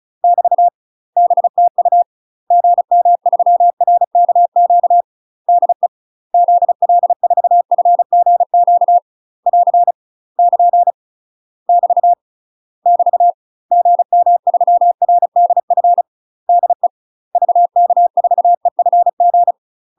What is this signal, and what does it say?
Morse code, heard as a keyed tone, about 36 wpm, 700 Hz. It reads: <BT> BTU GM3RKQ DE ZL4FGQ <AR> <KN> <BT> <BT> GM3RDF DE VK4EFG
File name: UK MASTER QSOs TRAINING / 35WPM / UK MASTER QSOs 0017